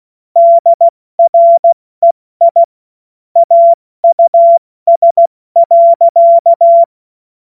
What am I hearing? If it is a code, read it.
Morse code, heard as a keyed tone, about 16 wpm, 680 Hz. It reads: DREI AUS.
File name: Morsekurs / Witze / Witz 14. Dezember 2021, 70 F80